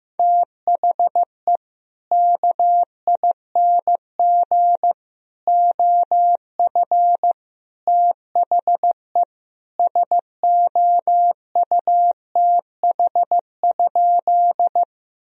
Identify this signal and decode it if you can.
Morse code, heard as a keyed tone, about 15 wpm, 700 Hz. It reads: THE KING OF THE SOUTH?
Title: Sentences from Top 1000 Words - Review Part 2 - 15wpm